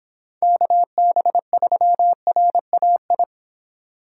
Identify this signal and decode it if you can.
Morse code, heard as a keyed tone, about 24 wpm, 700 Hz. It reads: KB3RAI